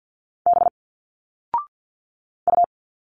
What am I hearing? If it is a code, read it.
Morse code, heard as a keyed tone, about 51 wpm, 700 Hz. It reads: B U
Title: Single Letters 50wpm